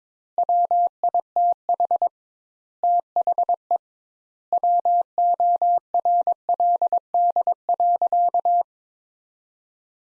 Morse code, heard as a keyed tone, about 22 wpm, 700 Hz. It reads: WITH THE WORLD.
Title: Sentences Top 200 Words - 22wpm